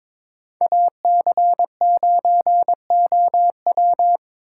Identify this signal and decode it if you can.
Morse code, heard as a keyed tone, about 22 wpm, 700 Hz. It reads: AC9OW
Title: USA Call Signs - 5 Characters - 3x Character Spacing - 22wpm